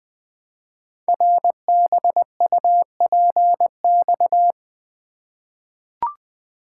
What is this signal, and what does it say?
Morse code, heard as a keyed tone, about 20 wpm, 700 Hz. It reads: RBUPX
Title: Code Groups - 5 Letters - 20wpm